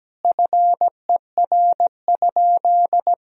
Morse code, heard as a keyed tone, about 17 wpm, 700 Hz. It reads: FER?
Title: Sentences from Top 200 Words and Abbreviations 17wpm